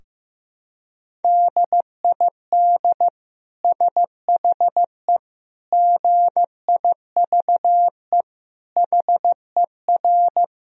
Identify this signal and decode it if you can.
Morse code, heard as a keyed tone, about 15 wpm, 700 Hz. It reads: DID SHE GIVE HER
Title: Sentences from Top 1000 Words - Review Part 4 - 15wpm